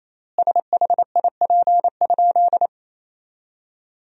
Morse code, heard as a keyed tone, about 28 wpm, 700 Hz. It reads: SHIP?
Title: 3x Word-Spacing - Sentences from Top 400 Words - 28wpm